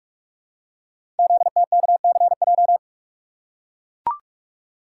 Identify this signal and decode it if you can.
Morse code, heard as a keyed tone, about 44 wpm, 700 Hz. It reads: ZTKCJ
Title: Code Groups - 5 Letters - 45wpm